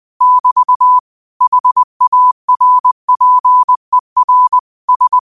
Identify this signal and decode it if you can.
Morse code, heard as a keyed tone, about 20 wpm, 1000 Hz. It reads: <BT> HARPER S
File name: qotdfiles / 20wpm